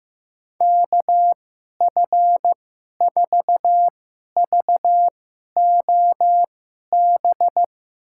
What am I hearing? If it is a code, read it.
Morse code, heard as a keyed tone, about 15 wpm, spaced as for 11 wpm, 700 Hz. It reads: KF4VOB
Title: USA Call Signs - 6 Characters - 2x Character Spacing - 15wpm